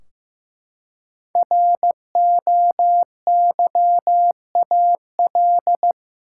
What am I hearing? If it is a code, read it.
Morse code, heard as a keyed tone, about 15 wpm, 700 Hz. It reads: ROYAL
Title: Five Letter Words - 15wpm